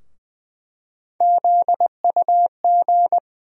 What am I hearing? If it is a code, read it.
Morse code, heard as a keyed tone, about 20 wpm, 700 Hz. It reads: ZUG
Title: USA Call Sign Postfixes - 3 Letters 20wpm